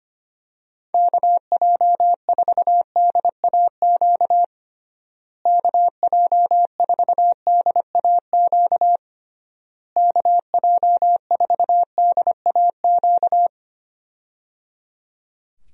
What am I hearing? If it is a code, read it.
Morse code, heard as a keyed tone, about 25 wpm, 700 Hz. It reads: KJ4DAQ KJ4DAQ KJ4DAQ
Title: USA Call Signs - 6 Characters - Repeated 3x - 25wpm